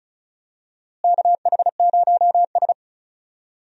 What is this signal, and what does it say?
Morse code, heard as a keyed tone, about 34 wpm, 700 Hz. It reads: KH0S
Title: USA Call Signs - 4 Characters - Repeated 3x - 35wpm